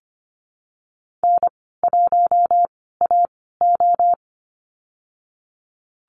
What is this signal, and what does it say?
Morse code, heard as a keyed tone, about 25 wpm, spaced as for 16 wpm, 700 Hz. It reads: N1AO
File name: Lesson 15g - Number 5 - Call Signs 25wpm